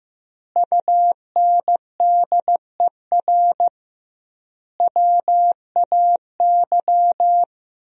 Morse code, heard as a keyed tone, about 15 wpm, 700 Hz. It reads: UNDER WAY
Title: 2x Word-Spacing - Sentences from Top 300 Words - 15wpm